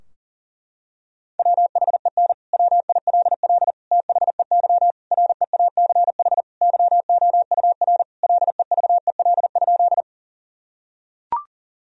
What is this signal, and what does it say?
Morse code, heard as a keyed tone, about 40 wpm, 700 Hz. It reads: WHEN WILL THEY REACH YOUR LEVEL?
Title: Sentences from Top 1000 Words - Review Part 4 - 40wpm